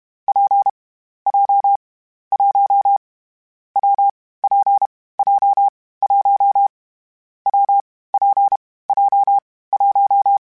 Morse code, heard as a keyed tone, about 32 wpm, 800 Hz. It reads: P J 1 W P J 1 W P J 1
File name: Variable Speed / Variable Speed-mp3s / WPJ WPJ1 sp @30